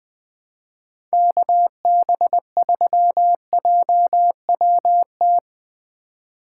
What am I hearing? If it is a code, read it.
Morse code, heard as a keyed tone, about 20 wpm, 700 Hz. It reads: KB3JWT
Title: USA Call Signs - 6 Characters - 2x Character Spacing - 20wpm